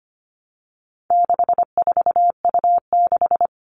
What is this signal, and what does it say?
Morse code, heard as a keyed tone, about 25 wpm, 700 Hz. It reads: 64U6
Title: All Cut Numbers - 4-Digit 25wpm